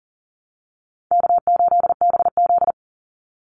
Morse code, heard as a keyed tone, about 40 wpm, 700 Hz. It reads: K8BZ
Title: Easy Call Signs Repeated 3x - 40wpm